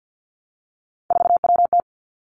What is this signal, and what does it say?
Morse code, heard as a keyed tone, about 50 wpm, 700 Hz. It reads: 4WT
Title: Three Challenging Letter-Number Combinations 50wpm